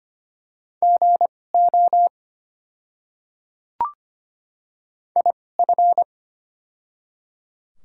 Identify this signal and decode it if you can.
Morse code, heard as a keyed tone, about 25 wpm, spaced as for 18 wpm, 700 Hz. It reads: GO IF
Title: 2x Character Spacing - Top 100 Words - 25wpm